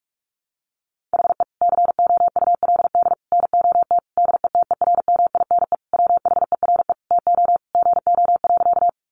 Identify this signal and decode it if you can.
Morse code, heard as a keyed tone, about 44 wpm, 700 Hz. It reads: HE COULD NOT DETERMINE WHERE TO GO.
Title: Sentences from Top 700 Words 45wpm